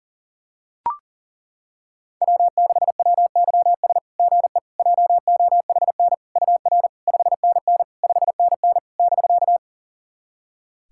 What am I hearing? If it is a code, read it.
Morse code, heard as a keyed tone, about 40 wpm, 690 Hz. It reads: W6WYS GE JOHN UR 5NN 5NN <BK>
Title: SOTA QSO Practice 40wpm